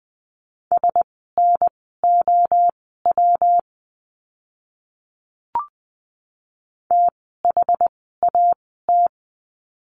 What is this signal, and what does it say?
Morse code, heard as a keyed tone, about 20 wpm, spaced as for 15 wpm, 700 Hz. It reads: SNOW THAT
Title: 115 Lesson 19f - Letter w - Words 20wpm